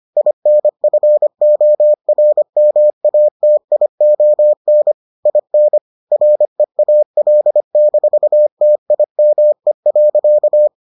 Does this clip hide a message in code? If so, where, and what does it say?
Yes: Morse code from 0.2 to 10.7 s, heard as a keyed tone, about 25 wpm, 600 Hz. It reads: INFORMATION IN REAL-TIME.